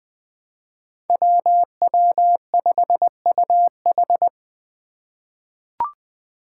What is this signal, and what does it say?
Morse code, heard as a keyed tone, about 20 wpm, 700 Hz. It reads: WW5UH
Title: Five Challenging Letter-Number Combinations 20wpm